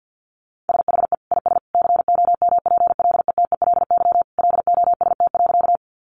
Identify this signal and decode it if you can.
Morse code, heard as a keyed tone, about 50 wpm, 700 Hz. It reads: SHE IS COMPLETELY LOST.
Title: Additional Easy Phrases 50wpm